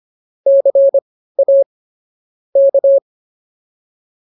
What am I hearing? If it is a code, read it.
Morse code, heard as a keyed tone, about 24 wpm, spaced as for 15 wpm, 550 Hz. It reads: CA K